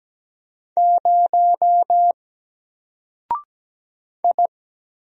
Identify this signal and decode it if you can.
Morse code, heard as a keyed tone, about 17 wpm, 700 Hz. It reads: 0 I
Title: Single Character (Letter:Number:Punctuation:Prosign) 17wpm